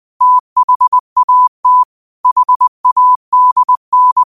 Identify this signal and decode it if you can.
Morse code, heard as a keyed tone, about 20 wpm, 1000 Hz. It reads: THAT HADN
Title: 20 WPM morse code quotes for Thu, 29 Feb 2024